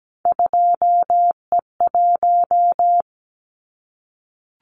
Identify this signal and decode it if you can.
Morse code, heard as a keyed tone, about 17 wpm, 700 Hz. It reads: 2E1
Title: Callsign Prefixes for Top 25 DXCC - Repeated 3x - 17wpm